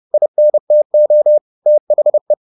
Morse code, heard as a keyed tone, about 30 wpm, 600 Hz. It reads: INTO THE
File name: the old man and the sea / the old man and the sea 1 30wpm